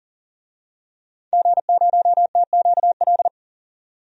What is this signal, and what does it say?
Morse code, heard as a keyed tone, about 40 wpm, 700 Hz. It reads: G0TQL
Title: DXCC-100 Call Signs - 5 Characters - 40wpm